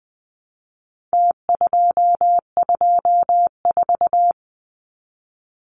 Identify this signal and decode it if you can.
Morse code, heard as a keyed tone, about 20 wpm, 700 Hz. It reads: T224
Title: Numbers - Contest Style - 4-Digit 20wpm